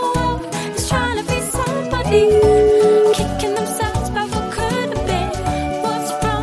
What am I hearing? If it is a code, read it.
Ringback tone (Europe, 425 Hz), with noise.